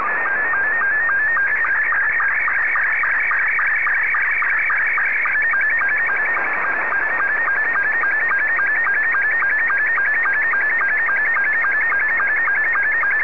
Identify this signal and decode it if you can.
SSTV (slow-scan television), Scottie 2.
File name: sstv